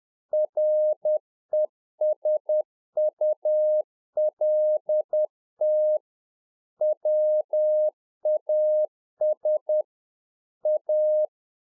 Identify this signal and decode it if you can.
Morse code, heard as a keyed tone, about 10 wpm, 610 Hz. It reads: RESULT WAS A